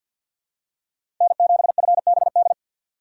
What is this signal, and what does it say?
Morse code, heard as a keyed tone, about 53 wpm, 700 Hz. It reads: N7FBD
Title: USA Call Signs - 5 Characters - 4x Character Spacing - 50wpm